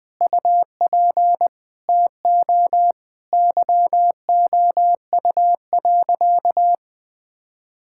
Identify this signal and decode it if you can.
Morse code, heard as a keyed tone, about 20 wpm, 700 Hz. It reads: UP TO YOU.
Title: Sentences with 7-Word Phrases - Spoken Beginning - 20wpm